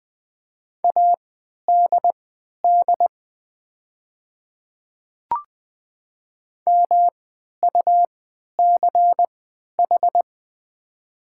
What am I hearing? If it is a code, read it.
Morse code, heard as a keyed tone, about 20 wpm, spaced as for 11 wpm, 700 Hz. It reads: ADD MUCH
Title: 3x Character Spacing - Top 300 Words - 20wpm